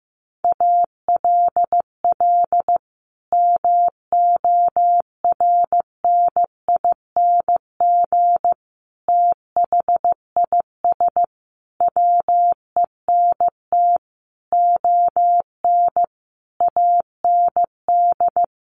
Morse code, heard as a keyed tone, about 15 wpm, 700 Hz. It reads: ALL MORNING THIS WENT ON AND
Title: Sentences from Top 1500 Words 15wpm